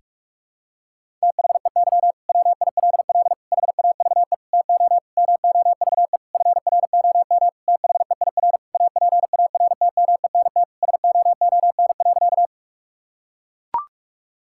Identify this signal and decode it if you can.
Morse code, heard as a keyed tone, about 42 wpm, 700 Hz. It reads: THEY WILL HAVE TO MOVE FROM THEIR APARTMENT SOON.
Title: Sentences from Top 1000 Words - Review Part 6 - 45wpm